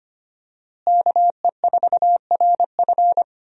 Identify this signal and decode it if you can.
Morse code, heard as a keyed tone, about 25 wpm, 700 Hz. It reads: KE4RF
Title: USA Call Signs - 5 Characters - Repeated 3x - 25wpm